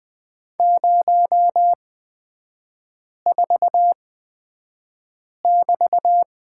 Morse code, heard as a keyed tone, about 20 wpm, spaced as for 5 wpm, 700 Hz. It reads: 04<BT>
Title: 224 Lesson 36e - BT - Three Characters 20wpm